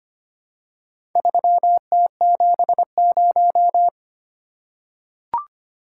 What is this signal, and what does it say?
Morse code, heard as a keyed tone, about 25 wpm, 700 Hz. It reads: 3T70
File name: practice / 25wpm / Four Letter-Number Combinations - 25wpm